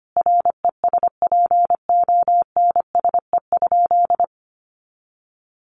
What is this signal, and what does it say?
Morse code, heard as a keyed tone, about 25 wpm, 700 Hz. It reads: RESPONSE?
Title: Sentences from Top 1400 Words 25wpm